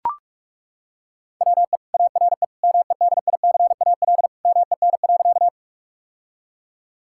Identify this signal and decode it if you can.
Morse code, heard as a keyed tone, about 42 wpm, 700 Hz. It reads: WE ARE MEDICAL MEN.
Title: Sentences from Top 1000 Words - Review Part 5 - 45wpm